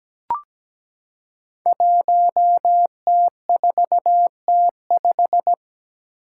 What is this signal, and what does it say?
Morse code, heard as a keyed tone, about 17 wpm, 700 Hz. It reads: 1T4T5
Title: Five Challenging Letter-Number Combinations 17wpm